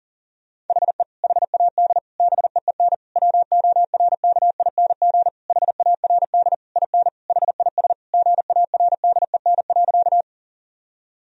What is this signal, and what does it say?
Morse code, heard as a keyed tone, about 37 wpm, 700 Hz. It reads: HE HAD BEEN WORKING HARD IN HIS GARDEN.
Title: Sentences from Top 1000 Words - Review Part 4 - 40wpm